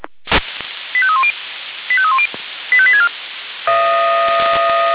Five-tone selective call (ZVEI), heard as quick runs of five tones, tones of about 70 ms. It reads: 85310 85310 85864